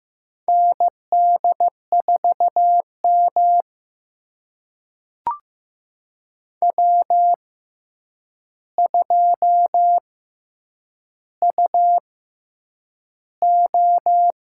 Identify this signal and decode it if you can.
Morse code, heard as a keyed tone, about 15 wpm, spaced as for 8 wpm, 700 Hz. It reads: ND4M W2UO
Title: USA Call Signs - 6x Character Spacing - 15wpm